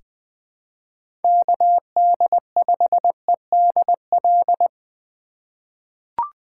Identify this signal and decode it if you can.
Morse code, heard as a keyed tone, about 20 wpm, 700 Hz. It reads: KD5EDL
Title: 2023 POTA Hunter Call Signs - 20wpm